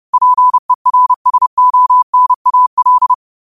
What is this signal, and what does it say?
Morse code, heard as a keyed tone, about 30 wpm, 1000 Hz. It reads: PERSONAL
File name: qotdfiles / 30wpm